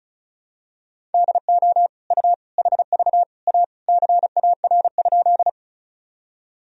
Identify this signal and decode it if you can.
Morse code, heard as a keyed tone, about 34 wpm, 700 Hz. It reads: DO U HV A CAR?
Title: Sentences from Top 300 Words and Abbreviations 35wpm